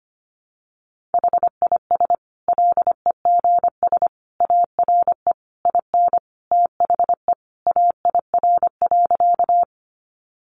Morse code, heard as a keyed tone, about 25 wpm, 700 Hz. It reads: HIS LEGS ARE IN THE AIR.
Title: Additional Easy Phrases 25wpm